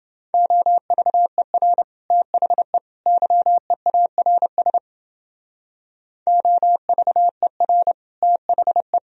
Morse code, heard as a keyed tone, about 29 wpm, 700 Hz. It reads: OVER THE YEARS OVER THE